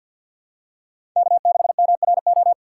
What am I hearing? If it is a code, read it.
Morse code, heard as a keyed tone, about 53 wpm, 700 Hz. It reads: K6MRO
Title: USA Call Signs - 5 Characters - 6x Character Spacing - 50wpm